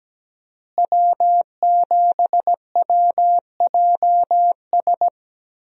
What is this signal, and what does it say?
Morse code, heard as a keyed tone, about 17 wpm, 700 Hz. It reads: W7WJS
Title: USA Call Signs - 5 Characters - Repeated 3x - 17wpm